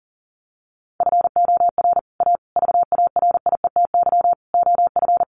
Morse code, heard as a keyed tone, about 37 wpm, 700 Hz. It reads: FOR A VARIETY OF
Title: Sets of 5 Words Encore 40wpm